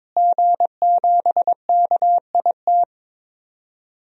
Morse code, heard as a keyed tone, about 22 wpm, 700 Hz. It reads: G7KIT
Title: DXCC-100 Call Signs - 5 Characters - Repeated 2x - 22wpm